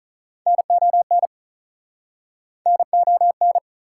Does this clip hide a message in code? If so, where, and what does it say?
Yes: Morse code from 0.5 to 3.6 s, heard as a keyed tone, about 47 wpm, 700 Hz. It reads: NON NON